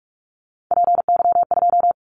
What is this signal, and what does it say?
Morse code, heard as a keyed tone, about 44 wpm, 700 Hz. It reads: PY2
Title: Callsign Prefixes for Top 50 DXCC - 45wpm